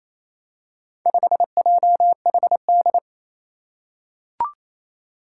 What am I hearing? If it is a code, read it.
Morse code, heard as a keyed tone, about 28 wpm, 700 Hz. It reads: 5JHD
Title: Four Challenging Letter-Number Combinations 28wpm